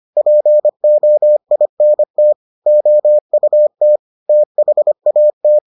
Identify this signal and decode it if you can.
Morse code, heard as a keyed tone, about 25 wpm, 600 Hz. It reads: POINT OUT THAT